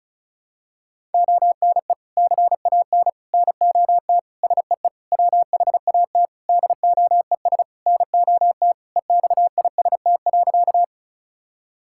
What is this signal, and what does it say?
Morse code, heard as a keyed tone, about 36 wpm, 700 Hz. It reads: ONE CAN NOT SEE WHAT DOES NOT EXIST.
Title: Sentences from Top 1000 Words - Review Part 5 - 35wpm